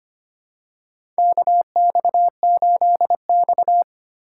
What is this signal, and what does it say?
Morse code, heard as a keyed tone, about 25 wpm, 700 Hz. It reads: KX8X
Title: K1USN SST QSO Practice - 25wpm